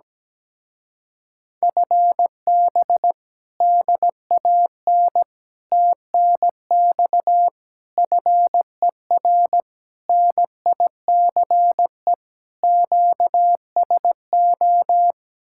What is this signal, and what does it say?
Morse code, heard as a keyed tone, about 17 wpm, 700 Hz. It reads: FB DAN TNX FER NICE QSO
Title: Speed-Racing Common QSO Phrases Encore - 17wpm